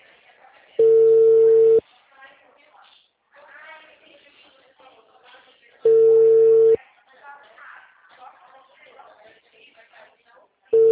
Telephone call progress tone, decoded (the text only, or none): ringback tone